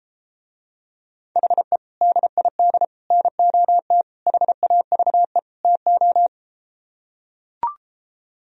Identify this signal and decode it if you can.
Morse code, heard as a keyed tone, about 33 wpm, 700 Hz. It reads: HE DID NOT HAVE TO